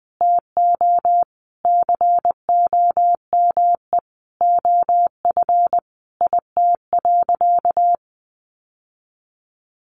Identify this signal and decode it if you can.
Morse code, heard as a keyed tone, about 20 wpm, 700 Hz. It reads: TO COME OF IT.
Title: Sentences from Top 1100 Words 20wpm